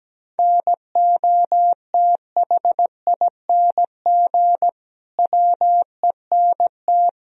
Morse code, heard as a keyed tone, about 17 wpm, 700 Hz. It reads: NOTHING WENT
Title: Sentences from Top 1000 Words - Review Part 2 - 17wpm